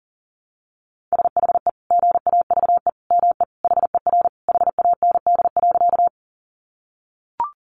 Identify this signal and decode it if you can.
Morse code, heard as a keyed tone, about 40 wpm, 700 Hz. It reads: SHE GAVE ME HER HAND.
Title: Sentences from Top 500 Words 40wpm